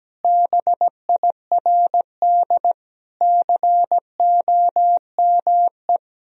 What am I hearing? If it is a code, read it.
Morse code, heard as a keyed tone, about 17 wpm, 700 Hz. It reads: BIRD COME
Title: Sentences from Top 1000 Words - Review Part 2 - 17wpm